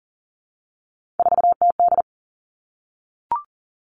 Morse code, heard as a keyed tone, about 40 wpm, 700 Hz. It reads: 4TD
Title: Three Challenging Letter-Number Combinations 40wpm